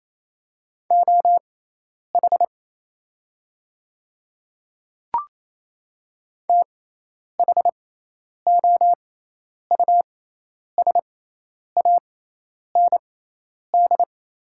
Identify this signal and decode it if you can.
Morse code, heard as a keyed tone, about 27 wpm, spaced as for 10 wpm, 690 Hz. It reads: OH THOUSAND
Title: 6x Character Spacing - Top 500 Words - 28wpm